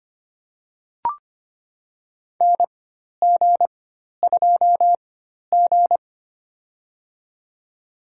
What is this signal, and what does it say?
Morse code, heard as a keyed tone, about 25 wpm, spaced as for 12 wpm, 700 Hz. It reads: NG2G
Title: USA Call Signs - 4x Character Spacing - 25wpm